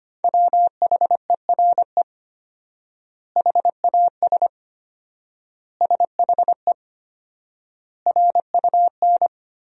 Morse code, heard as a keyed tone, about 25 wpm, 700 Hz. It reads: WHERE HAS SHE RUN
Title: 4x Word-Spacing - Sentences from Top 300 Words - 25wpm